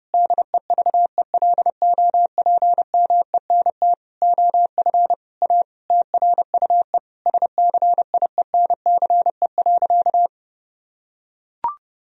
Morse code, heard as a keyed tone, about 32 wpm, 700 Hz. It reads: DEVELOPMENT OF A TRUE SCIENCE.
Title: Sentences from Top 1000 Words - Review Part 4 - 30wpm